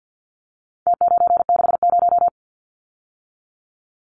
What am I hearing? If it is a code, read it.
Morse code, heard as a keyed tone, about 50 wpm, 700 Hz. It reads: T960
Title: Numbers - Contest Style - 4-Digit 50wpm